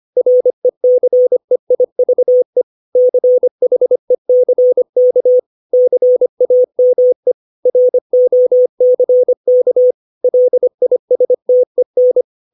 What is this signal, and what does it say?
Morse code, heard as a keyed tone, about 25 wpm, 500 Hz. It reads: RECEIVE CHECK CAME ROCK LISTEN
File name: 632 words / 632 Words 25WPM